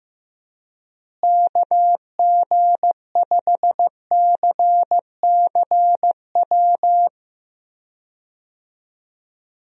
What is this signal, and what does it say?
Morse code, heard as a keyed tone, about 15 wpm, 700 Hz. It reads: KG5CCW
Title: USA Call Signs - 6 Characters - Repeated 3x - 15wpm